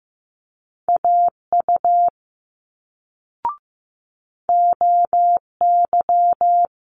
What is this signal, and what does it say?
Morse code, heard as a keyed tone, about 15 wpm, 700 Hz. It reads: AU OY
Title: Multi-letter Phonemes Quick Fire 15wpm